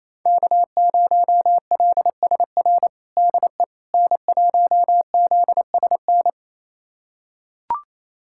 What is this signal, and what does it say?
Morse code, heard as a keyed tone, about 28 wpm, 700 Hz. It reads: K0LSR DE N1ZSN
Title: USA Callsign DE Callsign - 5 Characters - 28wpm